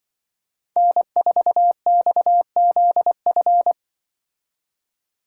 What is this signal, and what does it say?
Morse code, heard as a keyed tone, about 23 wpm, 700 Hz. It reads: N4XZF
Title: Speed Racing - USA 1x3 Call Signs 20wpm